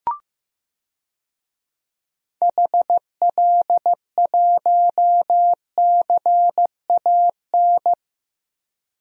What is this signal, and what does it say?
Morse code, heard as a keyed tone, about 15 wpm, 700 Hz. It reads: HL1CAN
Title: DXCC-100 Call Signs - 6 Characters - 15wpm